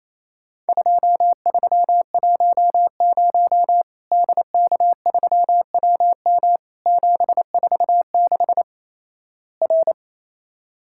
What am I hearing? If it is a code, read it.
Morse code, heard as a keyed tone, about 28 wpm, 700 Hz. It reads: 2310 DK3WM 746 R
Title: Worked All Europe DX Contest - QTC Exchange - 28wpm